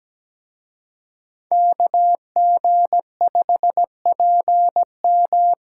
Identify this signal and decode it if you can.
Morse code, heard as a keyed tone, about 20 wpm, 700 Hz. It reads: KG5PM
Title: Speed Racing - USA 2x2 Call Signs 17wpm